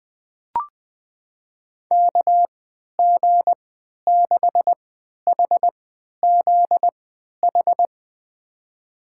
Morse code, heard as a keyed tone, about 21 wpm, spaced as for 11 wpm, 700 Hz. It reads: KG6HZH
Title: USA Call Signs - 6 Characters - 3x Character Spacing - 20wpm